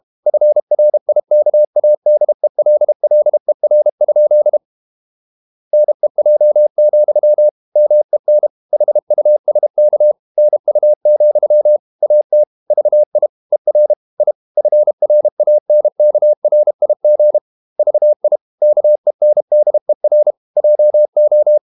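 Morse code, heard as a keyed tone, about 32 wpm, 600 Hz. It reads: FRIKADELLER? NEJ, MEN HUSK NU, AT VI ER I FRANKRIG VI KENDER JO
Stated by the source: High-speed QRQ CW at 32 WPM.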